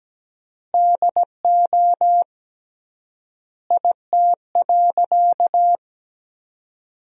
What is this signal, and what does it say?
Morse code, heard as a keyed tone, about 17 wpm, 700 Hz. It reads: DO IT.
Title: 3x Word-Spacing - Sentences from Top 100 Words - 17wpm